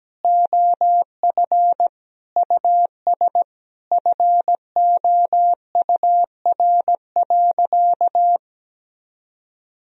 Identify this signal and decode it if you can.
Morse code, heard as a keyed tone, about 17 wpm, 700 Hz. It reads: OF US FOUR.
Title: Sentences from Top 1000 Words - Review Part 1 - 17wpm